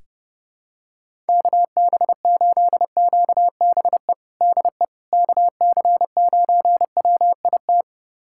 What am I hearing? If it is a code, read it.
Morse code, heard as a keyed tone, about 32 wpm, 700 Hz. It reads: KB8QBE DE KC9WIT
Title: USA Callsign DE Callsign 30wpm